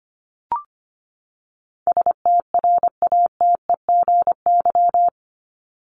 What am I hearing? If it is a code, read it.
Morse code, heard as a keyed tone, about 25 wpm, 700 Hz. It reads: STRATEGY
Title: Top 501-750 Words 25wpm